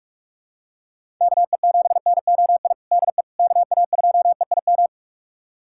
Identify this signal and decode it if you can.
Morse code, heard as a keyed tone, about 46 wpm, 700 Hz. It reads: KE7NOI DE KA2EIM
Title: USA Callsign DE Callsign - 6 Characters - 45wpm